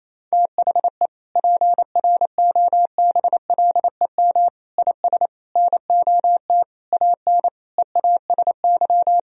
Morse code, heard as a keyed tone, about 28 wpm, 700 Hz. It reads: THE PROBLEM IS NOT AN EASY
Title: Sentences from Top 1000 Words - Review Part 2 - 28wpm